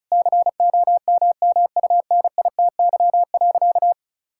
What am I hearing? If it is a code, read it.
Morse code, heard as a keyed tone, about 36 wpm, 700 Hz. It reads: COMMUNITY.
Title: Sentences from Top 1000 Words - Review Part 5 - 35wpm